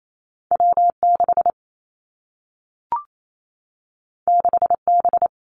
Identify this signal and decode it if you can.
Morse code, heard as a keyed tone, about 28 wpm, 700 Hz. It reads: W 6 6 B
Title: Two Challenging Letter-Number Combinations 28wpm